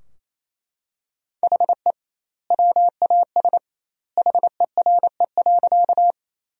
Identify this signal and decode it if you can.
Morse code, heard as a keyed tone, about 27 wpm, 700 Hz. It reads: HE WAS HERE.
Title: 2x Word-Spacing - Sentences from Top 200 Words - 28wpm